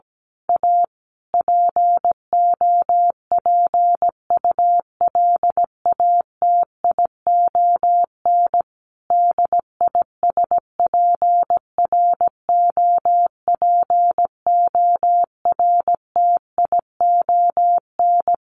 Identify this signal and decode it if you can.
Morse code, heard as a keyed tone, about 17 wpm, 700 Hz. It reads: A POPULATION DISPROPORTION
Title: A Sound Of Thunder 17wpm